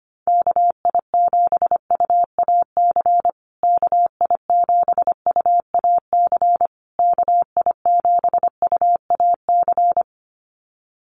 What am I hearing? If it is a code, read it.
Morse code, heard as a keyed tone, about 25 wpm, 700 Hz. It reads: KI7UAC KI7UAC KI7UAC
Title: USA Call Signs Repeated 3x 25wpm